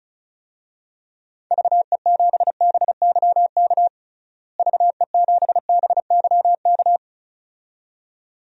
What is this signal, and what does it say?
Morse code, heard as a keyed tone, about 36 wpm, 700 Hz. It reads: VE7BYK VE7BYK
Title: DXCC-100 Call Signs - 6 Characters - Repeated 2x - 35wpm